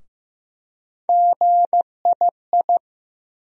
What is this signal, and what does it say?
Morse code, heard as a keyed tone, about 15 wpm, 700 Hz. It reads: GII
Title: USA Call Sign Postfixes - 3 Letters 15wpm